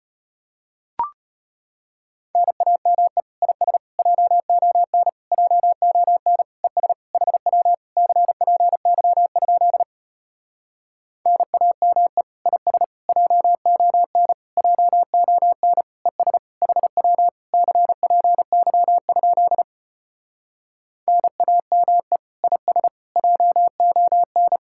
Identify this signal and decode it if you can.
Morse code, heard as a keyed tone, about 33 wpm, 700 Hz. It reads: NAME IS JON JON ES HW CPY? NAME IS JON JON ES HW CPY? NAME IS JON